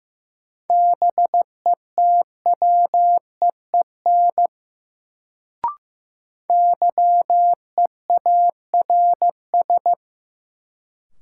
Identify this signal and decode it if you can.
Morse code, heard as a keyed tone, about 15 wpm, 700 Hz. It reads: BETWEEN YEARS
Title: Sets of 2 Words - New Word Drill - 15wpm